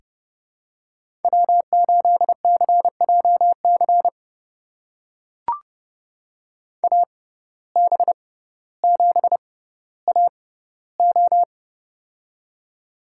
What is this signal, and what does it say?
Morse code, heard as a keyed tone, about 30 wpm, spaced as for 15 wpm, 700 Hz. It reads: W8CJC AB7AO
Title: USA Call Signs - 5 Characters - 6x Character Spacing - 30wpm